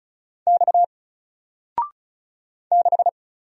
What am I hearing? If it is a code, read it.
Morse code, heard as a keyed tone, about 36 wpm, 700 Hz. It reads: X 6
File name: Single-Letter-Number-35wpm